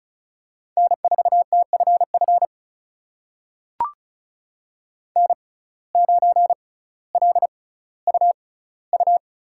Morse code, heard as a keyed tone, about 36 wpm, spaced as for 18 wpm, 700 Hz. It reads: N4TFF N9LUU
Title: USA Call Signs - 5 Characters - 6x Character Spacing - 35wpm